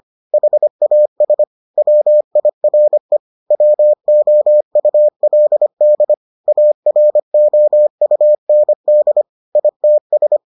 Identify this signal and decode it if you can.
Morse code, heard as a keyed tone, about 25 wpm, 600 Hz. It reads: HAS WIRE WOULD AROUND ITS